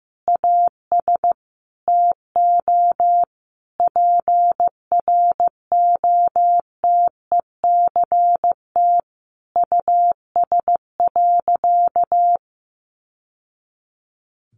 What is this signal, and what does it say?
Morse code, heard as a keyed tone, about 15 wpm, 700 Hz. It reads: AS TO PROTECT US.
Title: Sentences from Top 900 Words 15wpm